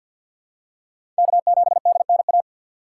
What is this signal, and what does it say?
Morse code, heard as a keyed tone, about 52 wpm, 700 Hz. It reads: K7DNA
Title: USA Call Signs - 5 Characters - 3x Character Spacing - 50wpm